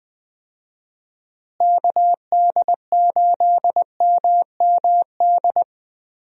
Morse code, heard as a keyed tone, about 20 wpm, 700 Hz. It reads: KD8MMD
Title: USA Call Signs - 6 Characters - Repeated 3x - 20wpm